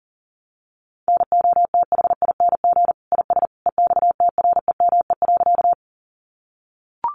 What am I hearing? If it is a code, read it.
Morse code, heard as a keyed tone, about 40 wpm, 700 Hz. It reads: NOTHING IS EXTREME.
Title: Sentences from Top 1300 Words 40wpm